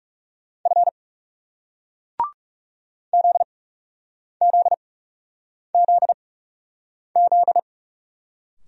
Morse code, heard as a keyed tone, about 40 wpm, spaced as for 8 wpm, 700 Hz. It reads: F ZZZZ